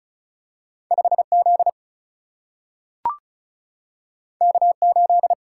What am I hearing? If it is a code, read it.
Morse code, heard as a keyed tone, about 36 wpm, 700 Hz. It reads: 5 Z K 8
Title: Single-Letter Single-Number 35wpm